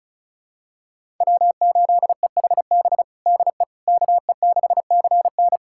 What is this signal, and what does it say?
Morse code, heard as a keyed tone, about 36 wpm, 700 Hz. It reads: W8EHB DE KE6CN
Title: USA Callsign DE Callsign - 5 Characters - 35wpm